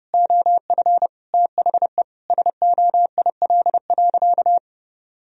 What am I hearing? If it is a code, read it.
Morse code, heard as a keyed tone, about 30 wpm, 700 Hz. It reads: OF THE SOIL.
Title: Sentences from Top 1000 Words - Review Part 4 - 30wpm